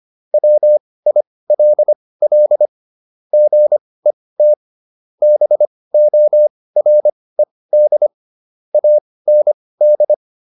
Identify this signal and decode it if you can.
Morse code, heard as a keyed tone, about 25 wpm, spaced as for 18 wpm, 600 Hz. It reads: WILL GET BORED AND